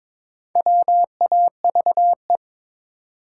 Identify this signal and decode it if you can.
Morse code, heard as a keyed tone, about 23 wpm, 700 Hz. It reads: WAVE
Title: Sentences from Top 1400 Words - New Word Drill Alt - 22wpm